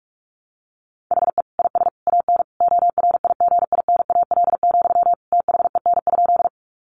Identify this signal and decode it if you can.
Morse code, heard as a keyed tone, about 44 wpm, 700 Hz. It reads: HE IS AN ORIGINAL, THEN?
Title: Sentences from Top 900 Words 45wpm